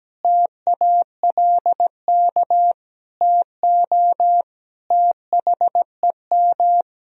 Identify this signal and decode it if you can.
Morse code, heard as a keyed tone, about 17 wpm, 700 Hz. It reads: TALK TO THEM
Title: Sentences from Top 1000 Words - Review Part 3 - 17wpm